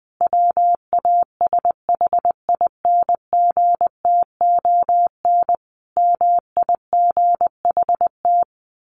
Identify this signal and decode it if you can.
Morse code, heard as a keyed tone, about 20 wpm, 700 Hz. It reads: WASHINGTON MIGHT
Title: A Sound Of Thunder 20wpm